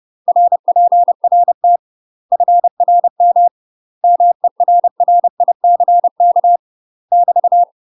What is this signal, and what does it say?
Morse code, heard as a keyed tone, about 30 wpm, 700 Hz. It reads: RPRT FRM MERRICK <BT>
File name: UK MASTER QSOs TRAINING / 30WPM / UK MASTER QSOs 0017